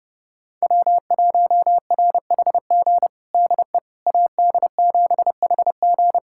Morse code, heard as a keyed tone, about 32 wpm, 700 Hz. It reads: W1RHG DE AD7HG
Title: USA Callsign DE Callsign - 5 Characters - 30wpm